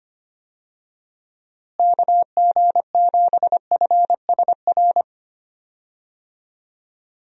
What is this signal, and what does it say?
Morse code, heard as a keyed tone, about 25 wpm, 700 Hz. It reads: KG7FSR
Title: USA Call Signs - 6 Characters - Repeated 2x - 25wpm